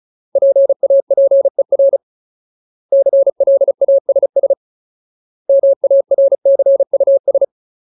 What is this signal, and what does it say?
Morse code, heard as a keyed tone, about 34 wpm, 550 Hz. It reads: PAPER CLASS MARCUS
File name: Ultimate Word List / 35WPM Ultimate